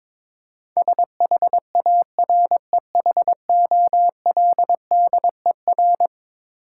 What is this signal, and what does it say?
Morse code, heard as a keyed tone, about 22 wpm, 700 Hz. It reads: SHAREHOLDER
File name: practice / 22wpm / Eleven Letter Words - 22wpm